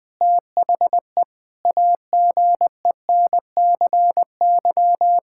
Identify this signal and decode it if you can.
Morse code, heard as a keyed tone, about 20 wpm, 700 Hz. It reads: THE AGENCY
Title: Sentences with 7-Word Phrases - 20wpm